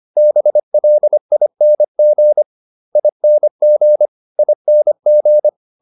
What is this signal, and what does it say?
Morse code, heard as a keyed tone, about 25 wpm, 600 Hz. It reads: BLING ING ING